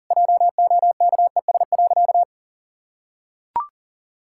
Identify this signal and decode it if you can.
Morse code, heard as a keyed tone, about 37 wpm, 700 Hz. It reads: JOKES.
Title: Pangram Challenge v2 - 40wpm